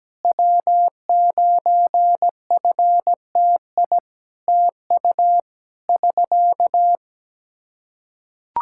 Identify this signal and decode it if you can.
Morse code, heard as a keyed tone, about 17 wpm, 700 Hz. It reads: W9FTI TU <SK>
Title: Practice QSO - Exchange 4 - Ending - 17wpm